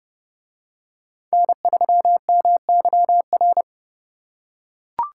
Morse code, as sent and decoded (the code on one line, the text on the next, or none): -. ...-- -- -.-- .-.
N3MYR